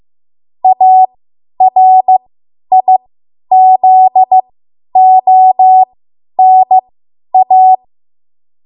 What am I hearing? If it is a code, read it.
Morse code, heard as a keyed tone, about 15 wpm, spaced as for 10 wpm, 750 Hz. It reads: ARIZONA
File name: Morse / 10WPM